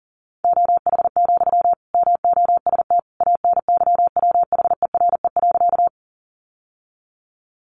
Morse code, heard as a keyed tone, about 40 wpm, 700 Hz. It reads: OH, MOST ANYWHERE.
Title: Sentences from Top 1400 Words 40wpm